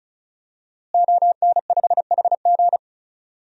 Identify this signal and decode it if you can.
Morse code, heard as a keyed tone, about 34 wpm, 700 Hz. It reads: ON5HG
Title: DXCC-100 Call Signs - 5 Characters - Repeated 3x - 35wpm